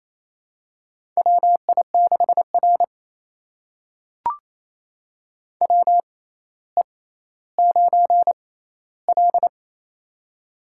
Morse code, heard as a keyed tone, about 27 wpm, spaced as for 14 wpm, 700 Hz. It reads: WI6R WE9L
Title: USA Call Signs - 4 Characters - 6x Character Spacing - 28wpm